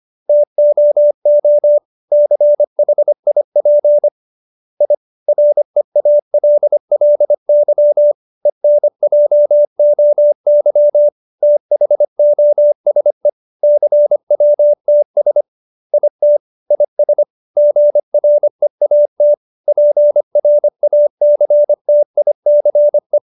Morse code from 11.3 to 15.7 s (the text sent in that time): THOSE CWTS